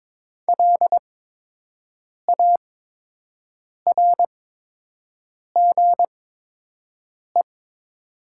Morse code, heard as a keyed tone, about 22 wpm, spaced as for 6 wpm, 700 Hz. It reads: LARGE
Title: 8x Character Spacing - Top 200 Words - 22wpm